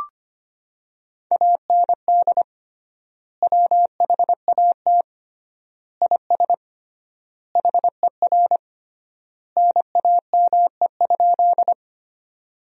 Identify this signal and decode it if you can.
Morse code, heard as a keyed tone, about 25 wpm, 700 Hz. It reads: AND WHAT IS HER NAME?
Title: 3x Word-Spacing - Sentences from Top 200 Words - 25wpm